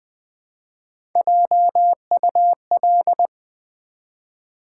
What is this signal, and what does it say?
Morse code, heard as a keyed tone, about 20 wpm, 700 Hz. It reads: JUL
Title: Build A Word - 1 Letter At A Time - 3 to 5-Letter Words - 20wpm